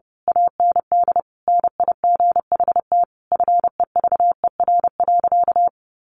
Morse code, heard as a keyed tone, about 30 wpm, 700 Hz. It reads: AND NIGHT FEVER.
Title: A Sound Of Thunder 30wpm